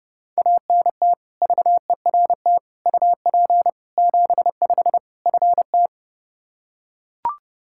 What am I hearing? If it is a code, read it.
Morse code, heard as a keyed tone, about 32 wpm, 700 Hz. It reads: ANT VERT UP 75 FT
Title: K1USN SST QSO Practice 30wpm